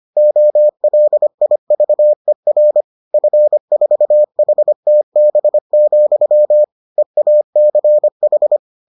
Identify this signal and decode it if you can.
Morse code, heard as a keyed tone, about 25 wpm, 600 Hz. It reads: OLIVER F4HTB, EACH